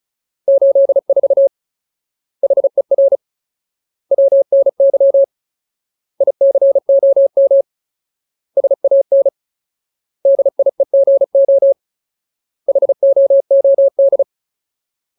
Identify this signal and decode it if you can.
Morse code, heard as a keyed tone, about 34 wpm, 550 Hz. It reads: 84 HER WNY ICOM SAN DIEGO HOOD
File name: Ultimate Word List / 35WPM Ultimate